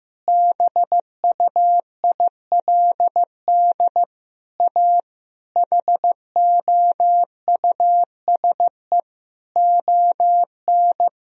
Morse code, heard as a keyed tone, about 16 wpm, 700 Hz. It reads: BUILD A HOUSE ON
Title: Sentences from Top 1000 Words - Review Part 4 - 15wpm